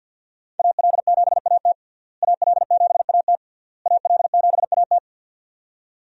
Morse code, heard as a keyed tone, about 52 wpm, 700 Hz. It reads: AL7AT AL7AT AL7AT
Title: USA Call Signs - 5 Characters - Repeated 3x - 50wpm